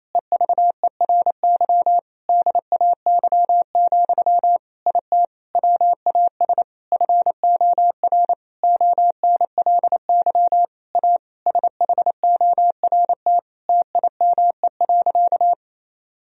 Morse code, heard as a keyed tone, about 28 wpm, 700 Hz. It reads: EVERY DAY, IT WAS FOR ONLY A SHORT TIME.
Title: Sentences from Top 1000 Words - Review Part 3 - 28wpm